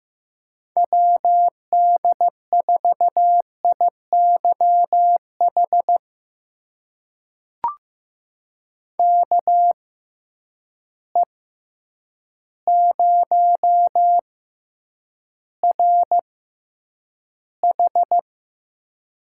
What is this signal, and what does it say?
Morse code, heard as a keyed tone, about 15 wpm, spaced as for 8 wpm, 700 Hz. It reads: WD4IYH KE0RH
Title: USA Call Signs - 6 Characters - 6x Character Spacing - 15wpm